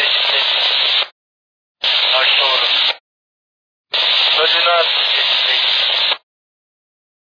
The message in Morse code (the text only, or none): none